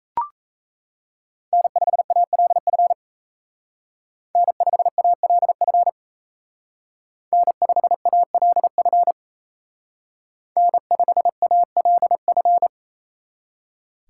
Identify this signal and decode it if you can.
Morse code, heard as a keyed tone, about 36 wpm, 700 Hz. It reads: N5ALF N5ALF N5ALF N5ALF